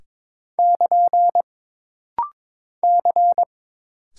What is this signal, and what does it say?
Morse code, heard as a keyed tone, about 22 wpm, 700 Hz. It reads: <KN> C
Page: This rapid-fire timing is geared to push your ICR (Instant Character Recognition) skills.